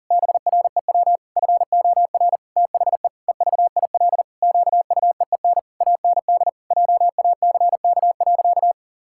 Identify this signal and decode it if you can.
Morse code, heard as a keyed tone, about 40 wpm, 700 Hz. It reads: BREW FOR THE EVIL QUEEN AND JACK.
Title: Pangram Challenge v2 - 40wpm